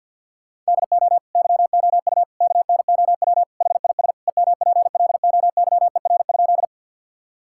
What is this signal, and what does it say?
Morse code, heard as a keyed tone, about 50 wpm, 700 Hz. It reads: DO YOU KNOW HIS EMPLOYER?
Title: Sentences from Top 1600 Words 50wpm